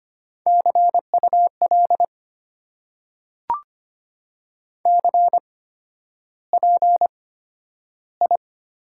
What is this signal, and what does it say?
Morse code, heard as a keyed tone, about 25 wpm, spaced as for 11 wpm, 690 Hz. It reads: CUL CPI
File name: Top 90 QSO Elements - 8x Character Spacing - 25wpm